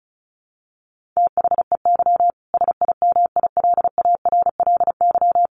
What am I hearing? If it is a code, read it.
Morse code, heard as a keyed tone, about 34 wpm, 700 Hz. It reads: THEY SIMILARLY
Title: Sentences from Top 1400 Words 35wpm